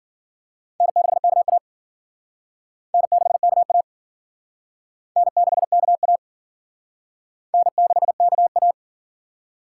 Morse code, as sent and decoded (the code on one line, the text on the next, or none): -. -.... -.- .- / -. -.... -.- .- / -. -.... -.- .- / -. -.... -.- .-
N6KA N6KA N6KA N6KA